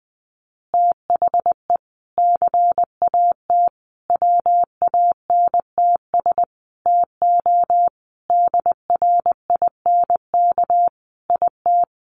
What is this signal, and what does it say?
Morse code, heard as a keyed tone, about 20 wpm, 700 Hz. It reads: THE CAT WANTS TO DRINK IT
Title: Additional Easy Phrases 20wpm